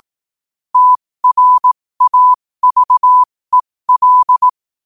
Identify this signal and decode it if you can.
Morse code, heard as a keyed tone, about 18 wpm, spaced as for 15 wpm, 1000 Hz. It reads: TRAVEL
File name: qotdfiles / 15wpm